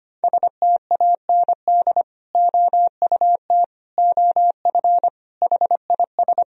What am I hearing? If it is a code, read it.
Morse code, heard as a keyed tone, about 25 wpm, 700 Hz. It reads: STAND OUT OF HIS
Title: Sentences from Top 1000 Words - Review Part 1 - 25wpm